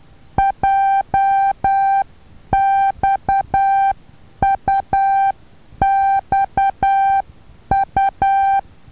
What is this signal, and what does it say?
Morse code, heard as a keyed tone, about 10 wpm, 780 Hz. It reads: JXUXU
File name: Taviro HA6VA B